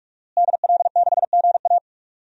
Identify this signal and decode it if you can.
Morse code, heard as a keyed tone, about 46 wpm, 700 Hz. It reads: DL6GA
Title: DXCC-100 Call Signs - 5 Characters - Repeated 2x - 45wpm